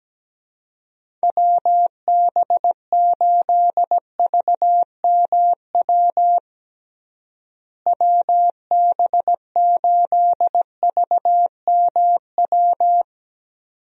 Morse code, heard as a keyed tone, about 17 wpm, 700 Hz. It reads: WB8VMW WB8VMW
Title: USA Call Signs - 6 Characters - Repeated 2x - 17wpm